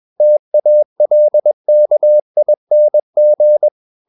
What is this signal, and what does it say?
Morse code, heard as a keyed tone, about 21 wpm, 600 Hz. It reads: TALKING
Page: CW med Gnister